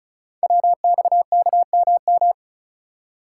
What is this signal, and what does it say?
Morse code, heard as a keyed tone, about 36 wpm, 700 Hz. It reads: WXKMM
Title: Code Groups - 5 Letters - 35wpm